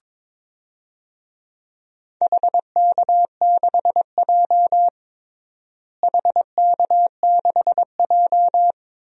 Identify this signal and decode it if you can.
Morse code, heard as a keyed tone, about 22 wpm, 700 Hz. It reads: HK6J HK6J
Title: DXCC-100 Call Signs - 4 Characters - Repeated 3x - 22wpm